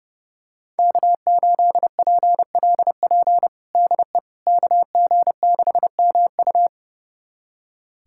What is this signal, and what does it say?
Morse code, heard as a keyed tone, about 30 wpm, 700 Hz. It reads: K8PLP DE KG6MU
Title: USA Callsign DE Callsign - 5 Characters - 30wpm